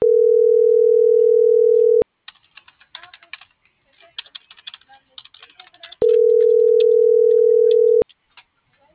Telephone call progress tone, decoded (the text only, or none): ringback tone